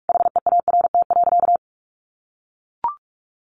Morse code, heard as a keyed tone, about 44 wpm, 700 Hz. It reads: HEART.
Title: Sentences from Top 1200 Words 45wpm